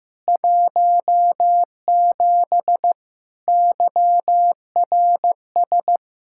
Morse code, heard as a keyed tone, about 16 wpm, 700 Hz. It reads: 17 YRS
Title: Practice QSO - Exchange 3 - More Info - 15wpm